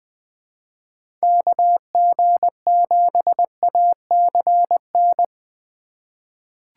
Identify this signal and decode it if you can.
Morse code, heard as a keyed tone, about 20 wpm, 700 Hz. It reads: KG7ACN
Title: USA Call Signs - 6 Characters - 2x Character Spacing - 20wpm